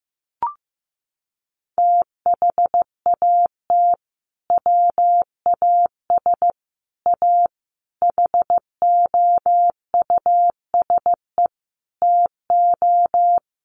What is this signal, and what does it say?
Morse code, heard as a keyed tone, about 15 wpm, 700 Hz. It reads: THAT WAS A HOUSE TO
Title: Sentences from Top 200 Words 15wpm